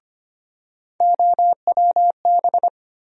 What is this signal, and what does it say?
Morse code, heard as a keyed tone, about 25 wpm, 700 Hz. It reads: OWB
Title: USA Call Sign Postfixes - 3 Letters 25wpm